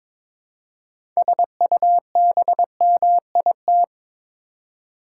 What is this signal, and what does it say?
Morse code, heard as a keyed tone, about 21 wpm, 700 Hz. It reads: SUBMIT
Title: Six Letter Words - 22wpm